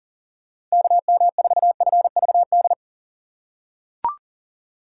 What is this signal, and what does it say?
Morse code, heard as a keyed tone, about 40 wpm, 700 Hz. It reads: KM4FVD
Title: USA Call Signs - 6 Characters - 6x Character Spacing - 40wpm